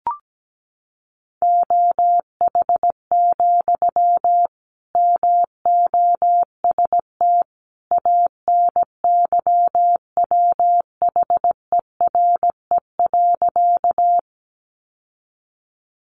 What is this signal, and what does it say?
Morse code, heard as a keyed tone, about 17 wpm, 700 Hz. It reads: OH, MOST ANYWHERE.
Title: Sentences from Top 1400 Words 17wpm